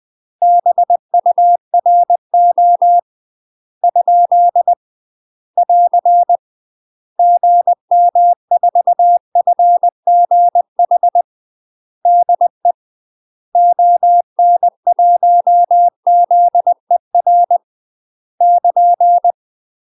Morse code, heard as a keyed tone, about 20 wpm, 700 Hz. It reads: BURO ? <AR> GM4FGH DE ON1ZER <KN>
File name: UK MASTER QSOs TRAINING / 20WPM / UK MASTER QSOs 0042